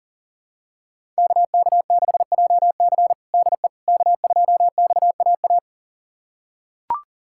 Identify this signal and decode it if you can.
Morse code, heard as a keyed tone, about 37 wpm, 700 Hz. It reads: KK6JC DE K2XAA
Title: USA Callsign DE Callsign - 5 Characters - 40wpm